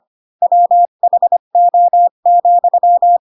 Morse code, heard as a keyed tone, about 25 wpm, 700 Hz. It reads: WHO,
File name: Pinocchio / Pinocchio ch3 20 25 ik1hge